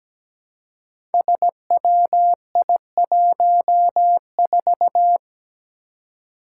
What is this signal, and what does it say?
Morse code, heard as a keyed tone, about 17 wpm, 700 Hz. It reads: SWI14
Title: Five Challenging Letter-Number Combinations 17wpm